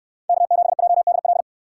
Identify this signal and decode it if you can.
Morse code, heard as a keyed tone, about 61 wpm, 700 Hz. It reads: K7JDL